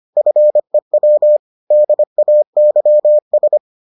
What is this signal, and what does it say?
Morse code, heard as a keyed tone, about 25 wpm, 600 Hz. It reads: FEW DAYS